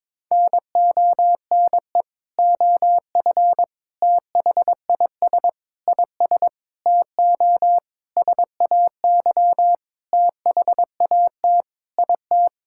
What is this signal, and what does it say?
Morse code, heard as a keyed tone, about 22 wpm, 700 Hz. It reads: NONE OF THIS IS TO SAY THAT IT
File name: Sentences with 7-Word Phrases - 22wpm